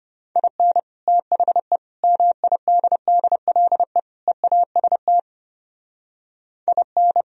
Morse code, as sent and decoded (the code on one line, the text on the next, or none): .. -. / - .... / -- .. -.. -.. .-.. / . .- ... - / .. -.
IN THE MIDDLE EAST IN